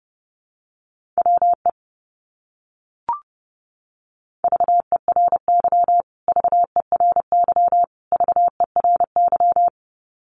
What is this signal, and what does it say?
Morse code, heard as a keyed tone, about 30 wpm, 700 Hz. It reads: WE VERY VERY VERY
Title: Top 200 Words Repeated 30wpm